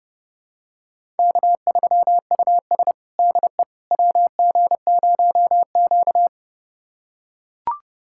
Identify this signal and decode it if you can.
Morse code, heard as a keyed tone, about 32 wpm, 700 Hz. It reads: K3US DE WG0Q
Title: USA Callsign DE Callsign - 4 Characters - 30wpm